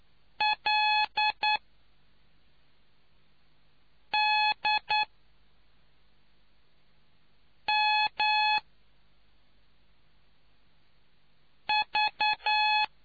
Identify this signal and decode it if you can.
Morse code, heard as a keyed tone, about 9 wpm, 850 Hz. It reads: L D M V